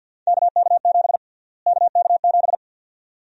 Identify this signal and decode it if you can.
Morse code, heard as a keyed tone, about 52 wpm, 700 Hz. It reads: KK7 KK7
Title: USA Call Sign Prefixes - Repeated 2x - 50wpm